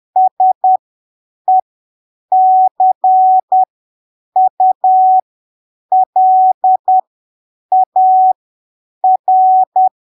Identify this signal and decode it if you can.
Morse code, heard as a keyed tone, about 10 wpm, spaced as for 7 wpm, 750 Hz. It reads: SECULAR
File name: arquivos / texto / texto 27 10ppm